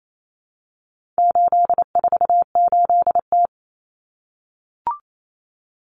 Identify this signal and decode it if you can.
Morse code, heard as a keyed tone, about 28 wpm, 700 Hz. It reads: 848T
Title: Common Cut Numbers - 4-Digit 28wpm